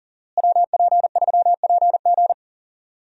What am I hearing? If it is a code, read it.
Morse code, heard as a keyed tone, about 38 wpm, 700 Hz. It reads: WP3PG
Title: USA Call Signs - 5 Characters - Repeated 3x - 40wpm